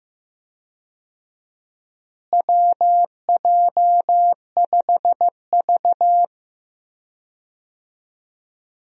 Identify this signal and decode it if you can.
Morse code, heard as a keyed tone, about 15 wpm, 700 Hz. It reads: WJ5V
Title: USA Call Signs - 4 Characters - Repeated 3x - 15wpm